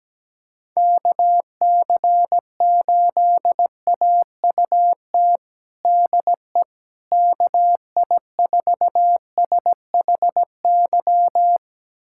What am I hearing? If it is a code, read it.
Morse code, heard as a keyed tone, about 17 wpm, 700 Hz. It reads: KC8AUT DE KI4SHY
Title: USA Callsign DE Callsign - 6 Characters - 17wpm